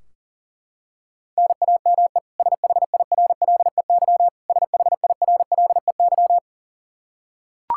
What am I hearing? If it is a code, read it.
Morse code, heard as a keyed tone, about 37 wpm, 700 Hz. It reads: NAME SHIRLEY SHIRLEY
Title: K1USN SST QSO Practice 40wpm